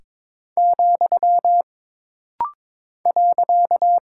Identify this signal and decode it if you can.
Morse code, heard as a keyed tone, about 22 wpm, 700 Hz. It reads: , .
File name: practice / 22wpm / Single Punctuation-Prosign - Rapid-Fire - 22wpm